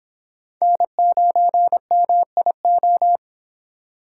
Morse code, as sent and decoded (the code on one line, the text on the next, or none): -. ----. -- .. ---
N9MIO